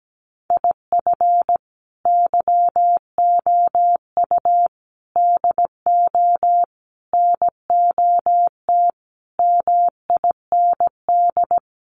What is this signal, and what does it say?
Morse code, heard as a keyed tone, about 17 wpm, 700 Hz. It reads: IF YOU DO NOT MIND
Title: Sets of 5 Words 17wpm